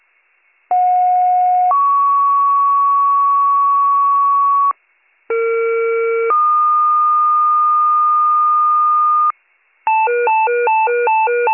Two-tone page, heard as a one-second tone, then a three-second tone, about 730 then 1120 Hz, about 460 then 1220 Hz. It is heard from 0.7 to 9.3 s.